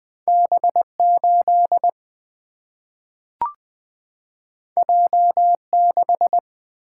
Morse code, heard as a keyed tone, about 20 wpm, 700 Hz. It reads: B8 J6
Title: Single-Letter Single-Number 20wpm